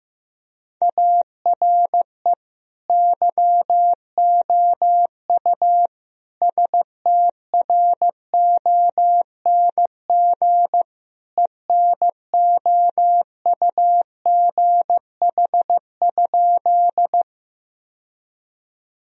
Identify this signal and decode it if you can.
Morse code, heard as a keyed tone, about 15 wpm, 700 Hz. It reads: ARE YOU STRONG ENOUGH?
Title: Sentences from Top 1000 Words - Review Part 3 - 15wpm